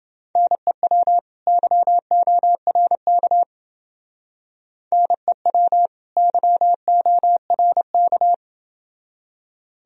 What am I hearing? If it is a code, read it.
Morse code, heard as a keyed tone, about 28 wpm, 700 Hz. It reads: NEW YORK NEW YORK